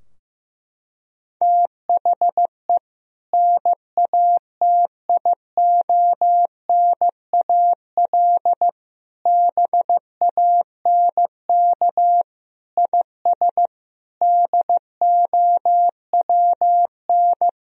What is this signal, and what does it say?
Morse code, heard as a keyed tone, about 15 wpm, 700 Hz. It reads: THE NATIONAL BANK IS DOWN
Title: Sentences from Top 1000 Words - Review Part 4 - 15wpm